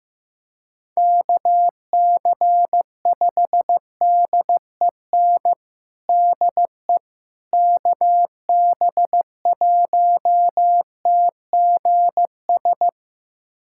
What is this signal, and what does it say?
Morse code, heard as a keyed tone, about 16 wpm, 700 Hz. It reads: KC5DEN DE KB1TGS
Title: USA Callsign DE Callsign - 6 Characters - 15wpm